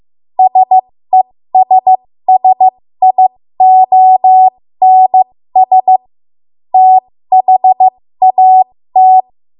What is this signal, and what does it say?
Morse code, heard as a keyed tone, about 15 wpm, spaced as for 13 wpm, 750 Hz. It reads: SESSIONS THAT
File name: Morse / 13WPM